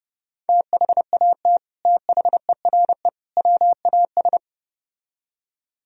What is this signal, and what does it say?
Morse code, heard as a keyed tone, about 33 wpm, 700 Hz. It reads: THAT THERE WAS